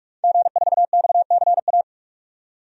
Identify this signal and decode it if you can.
Morse code, heard as a keyed tone, about 44 wpm, 690 Hz. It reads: G4XCA
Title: DXCC-100 Call Signs - 5 Characters - 45wpm